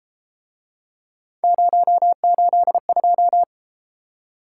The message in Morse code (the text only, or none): 082